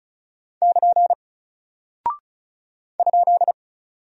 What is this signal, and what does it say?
Morse code, heard as a keyed tone, about 34 wpm, 700 Hz. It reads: <KN> ?
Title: Single Character (Letter:Number:Punctuation:Prosign) Alt 35wpm